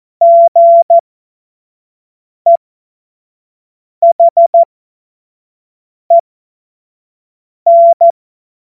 Morse code, heard as a keyed tone, about 14 wpm, spaced as for 5 wpm, 680 Hz. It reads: GEHEN
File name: Morsekurs / Witze / Witz 07.November 2023 25 F70